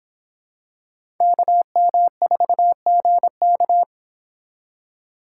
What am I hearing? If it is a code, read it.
Morse code, heard as a keyed tone, about 25 wpm, 700 Hz. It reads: KM4GK